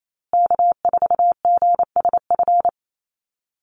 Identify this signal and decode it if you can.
Morse code, heard as a keyed tone, about 28 wpm, 700 Hz. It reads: K4GSF
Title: USA Call Signs Repeated 2x 28wpm